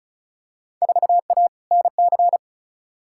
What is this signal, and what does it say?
Morse code, heard as a keyed tone, about 34 wpm, 700 Hz. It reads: 4A NC
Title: Field Day Exchanges Encore - 35wpm